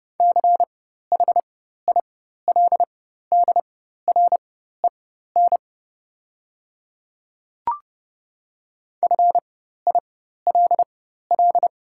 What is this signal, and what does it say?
Morse code, heard as a keyed tone, about 30 wpm, spaced as for 14 wpm, 690 Hz. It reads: CHILDREN FILL
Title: 4x Character Spacing - Top 500 Words - 30wpm